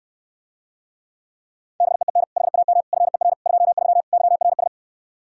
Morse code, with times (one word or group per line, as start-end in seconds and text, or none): 1.8-2.2 BEEN
2.4-2.8 HAM
2.9-3.3 FER
3.5-4.0 23
4.1-4.7 YRS